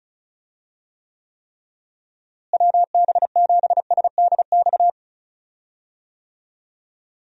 Morse code, heard as a keyed tone, about 34 wpm, 700 Hz. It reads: WB7SDX
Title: USA Call Signs - 6 Characters - Repeated 3x - 35wpm